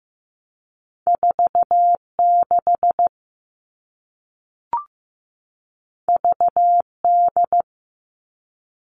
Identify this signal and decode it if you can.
Morse code, heard as a keyed tone, about 15 wpm, 700 Hz. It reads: 46 VD
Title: All Cut Numbers - 2-Digit 15wpm